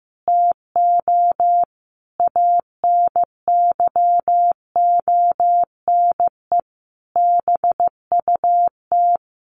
Morse code, heard as a keyed tone, about 15 wpm, 700 Hz. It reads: TO ANYONE BUT
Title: Sentences from Top 1300 Words 15wpm